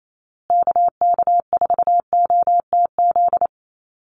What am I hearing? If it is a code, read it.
Morse code, heard as a keyed tone, about 28 wpm, 700 Hz. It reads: KK4OTZ
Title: USA Call Signs Repeated 3x 28wpm